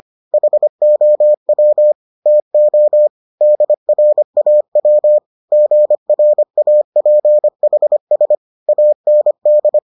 Morse code, heard as a keyed tone, about 25 wpm, 600 Hz. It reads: HOW TO DRAW GRAPHS AND